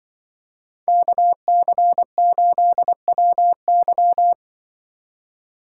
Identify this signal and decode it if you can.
Morse code, heard as a keyed tone, about 24 wpm, 700 Hz. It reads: KC8WY
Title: Speed Racing - USA 2x2 Call Signs 20wpm